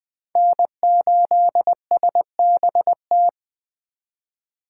Morse code, heard as a keyed tone, about 20 wpm, 700 Hz. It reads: N8SBT